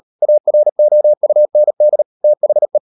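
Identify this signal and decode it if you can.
Morse code, heard as a keyed tone, about 38 wpm, 600 Hz. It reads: AROUND THE
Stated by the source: Morse code practice at 38 WPM, sending Ken Follett's novel Never in English, from CW med Gnister.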